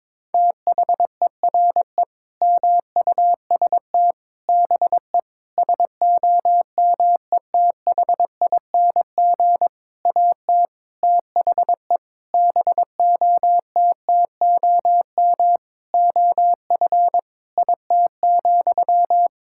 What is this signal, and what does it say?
Morse code, heard as a keyed tone, about 22 wpm, 700 Hz. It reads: THERE MUST BE SOMETHING AT THE BOTTOM OF IT,
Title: Sentences from Top 1000 Words - Review Part 6 - 22wpm